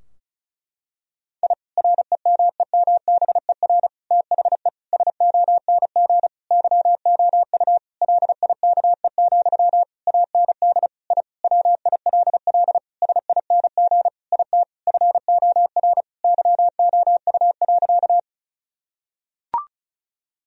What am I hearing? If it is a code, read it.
Morse code, heard as a keyed tone, about 36 wpm, 700 Hz. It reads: I REMEMBER THE SONG YOU LIKE, AND I WILL SING IT FOR YOU.
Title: Sentences from Top 1000 Words - Review Part 2 - 35wpm